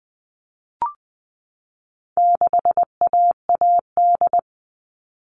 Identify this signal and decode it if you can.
Morse code, heard as a keyed tone, about 20 wpm, 700 Hz. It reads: 6AAD
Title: All Cut Numbers - 4-Digit 20wpm